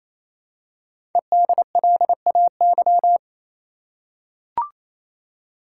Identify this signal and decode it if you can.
Morse code, heard as a keyed tone, about 27 wpm, 700 Hz. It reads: EDLAY
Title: Code Groups - 5 Letters - 28wpm